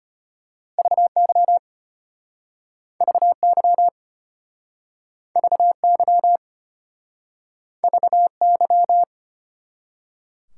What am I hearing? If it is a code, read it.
Morse code, heard as a keyed tone, about 33 wpm, 700 Hz. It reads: VY VY VY VY